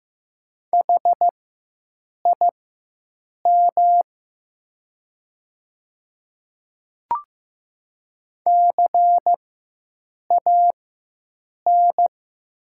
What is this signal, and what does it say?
Morse code, heard as a keyed tone, about 15 wpm, spaced as for 7 wpm, 700 Hz. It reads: HIM CAN
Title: 4x Character Spacing - Top 100 Words - 15wpm